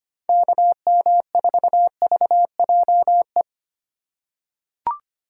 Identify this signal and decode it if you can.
Morse code, heard as a keyed tone, about 25 wpm, 700 Hz. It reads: KM4VJE
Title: USA Call Signs - 6 Characters - 4x Character Spacing - 25wpm